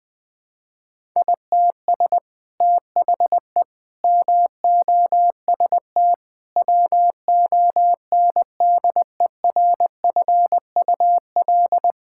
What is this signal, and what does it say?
Morse code, heard as a keyed tone, about 21 wpm, 700 Hz. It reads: ITS THE MOST WONDERFUL
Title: Sets of 8 Words - 20wpm